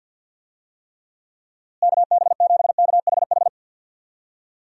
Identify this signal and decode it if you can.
Morse code, heard as a keyed tone, about 50 wpm, 700 Hz. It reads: KB7KHH
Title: USA Call Signs - 6 Characters - Repeated 2x - 50wpm